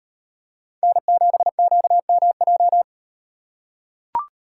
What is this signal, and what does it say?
Morse code, heard as a keyed tone, about 38 wpm, 700 Hz. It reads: N7QMJ